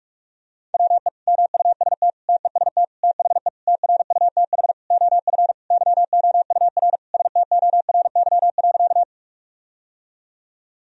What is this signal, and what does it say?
Morse code, heard as a keyed tone, about 46 wpm, 700 Hz. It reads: WE MUST TEST THE TRUTH OF YOUR STORY.
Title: Sentences from Top 1000 Words - Review Part 3 - 45wpm